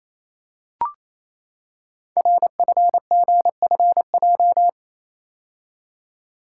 Morse code, heard as a keyed tone, about 28 wpm, 700 Hz. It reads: RFGFJ
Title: Code Groups - 5 Letters - 28wpm